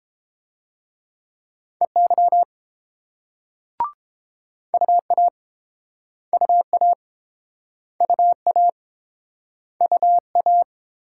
Morse code, heard as a keyed tone, about 30 wpm, 690 Hz. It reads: EY UA UA UA UA